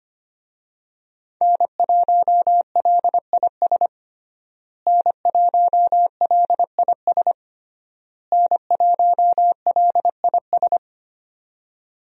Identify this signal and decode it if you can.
Morse code, heard as a keyed tone, about 25 wpm, 700 Hz. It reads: N1LIS N1LIS N1LIS
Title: USA Call Signs - 5 Characters - Repeated 3x - 25wpm